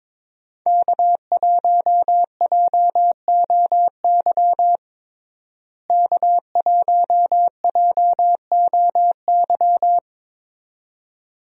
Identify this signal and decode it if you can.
Morse code, heard as a keyed tone, about 22 wpm, 700 Hz. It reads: K1JOY K1JOY
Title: USA Call Signs - 5 Characters - Repeated 2x - 22wpm